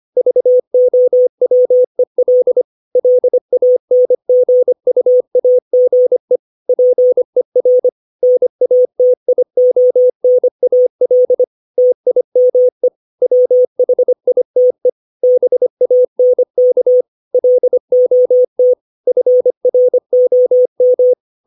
Morse code, heard as a keyed tone, about 25 wpm, 500 Hz. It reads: VOWEL LANGUAGE PER NATIONAL TIME WHITE BANK LOT FROM
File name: 632 words / 632 Words 25WPM